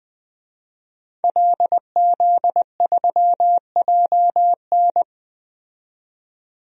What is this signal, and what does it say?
Morse code, heard as a keyed tone, about 21 wpm, 700 Hz. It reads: LZ3JN
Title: DXCC-100 Call Signs - 5 Characters - 20wpm